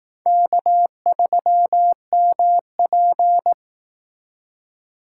Morse code, heard as a keyed tone, about 18 wpm, 700 Hz. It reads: K3MP
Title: Speed Racing - USA 1x2 Call Signs 15wpm